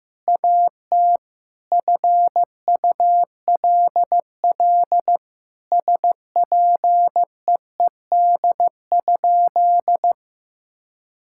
Morse code, heard as a keyed tone, about 16 wpm, 700 Hz. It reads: AT FULL SPEED?
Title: Sentences from Top 1000 Words - Review Part 5 - 15wpm